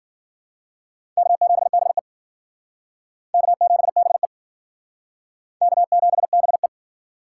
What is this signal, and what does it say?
Morse code, heard as a keyed tone, about 53 wpm, 700 Hz. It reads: K7BE K7BE K7BE